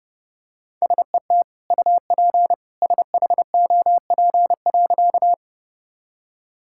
Morse code, heard as a keyed tone, about 28 wpm, 700 Hz. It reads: SET UP SHOP.
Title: Sentences from Top 1600 Words 30wpm